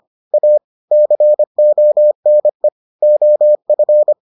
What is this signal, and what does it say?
Morse code, heard as a keyed tone, about 25 wpm, 600 Hz. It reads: A CONE OF